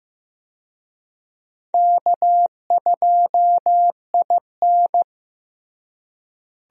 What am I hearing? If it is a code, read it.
Morse code, heard as a keyed tone, about 16 wpm, 700 Hz. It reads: K2IN
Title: USA Call Signs - 4 Characters - Repeated 2x - 15wpm